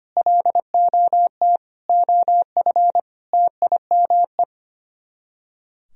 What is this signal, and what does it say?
Morse code, heard as a keyed tone, about 24 wpm, 700 Hz. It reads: LOT OF TIME
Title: Speed-Racing - Sets of 4 Words 25wpm